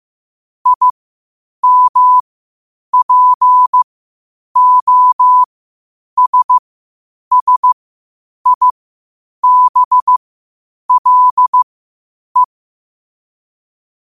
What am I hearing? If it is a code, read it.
Morse code, heard as a keyed tone, about 15 wpm, spaced as for 8 wpm, 1000 Hz. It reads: IMPOSSIBLE